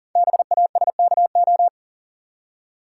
Morse code, heard as a keyed tone, about 38 wpm, 690 Hz. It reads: BASKO
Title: Code Groups - 5 Letters - 40wpm